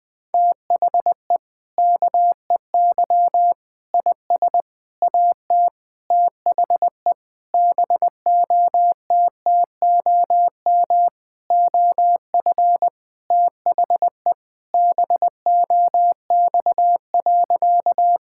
Morse code, heard as a keyed tone, about 20 wpm, 700 Hz. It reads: THE KEY IS AT THE BOTTOM OF THE BOX.
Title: Sentences with 5-Word Phrases - 20wpm